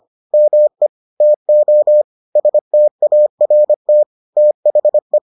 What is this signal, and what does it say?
Morse code, heard as a keyed tone, about 25 wpm, 600 Hz. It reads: ME TO START THE